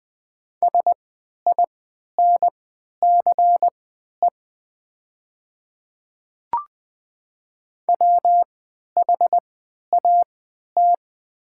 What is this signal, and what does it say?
Morse code, heard as a keyed tone, about 20 wpm, spaced as for 11 wpm, 700 Hz. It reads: SINCE WHAT
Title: 3x Character Spacing - Top 300 Words - 20wpm